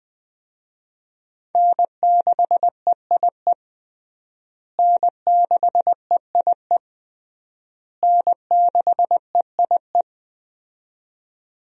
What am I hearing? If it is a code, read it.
Morse code, heard as a keyed tone, about 20 wpm, 700 Hz. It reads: N6EIE N6EIE N6EIE
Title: USA Call Signs - 5 Characters - Repeated 3x - 20wpm